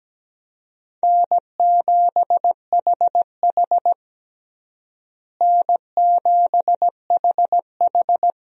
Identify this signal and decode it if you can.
Morse code, heard as a keyed tone, about 17 wpm, 700 Hz. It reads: N7HH N7HH
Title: USA Call Signs - 4 Characters - Repeated 2x - 17wpm